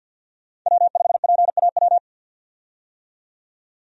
Morse code, heard as a keyed tone, about 49 wpm, 700 Hz. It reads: W5PAW
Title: USA Call Signs - 5 Characters - 50wpm